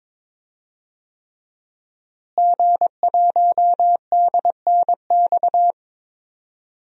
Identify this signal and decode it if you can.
Morse code, heard as a keyed tone, about 22 wpm, 700 Hz. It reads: G1DNX
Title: DXCC-100 Call Signs - 5 Characters - Repeated 3x - 22wpm